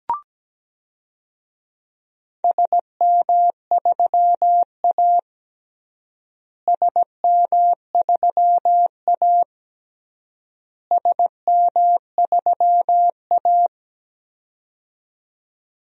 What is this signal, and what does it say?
Morse code, heard as a keyed tone, about 17 wpm, 700 Hz. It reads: SM3A SM3A SM3A
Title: DXCC-100 Call Signs - 4 Characters - Repeated 3x - 17wpm